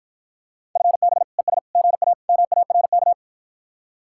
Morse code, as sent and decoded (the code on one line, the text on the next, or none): ..-. -... / . ... / --. .- / -- .- .-. -.-
FB ES GA MARK